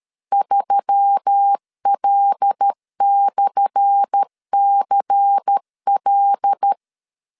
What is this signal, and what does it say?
Morse code, heard as a keyed tone, about 12 wpm, 800 Hz. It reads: 3L/CL
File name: LEZIONI / 10 SIGNS / 10C DOT 12wpm 2451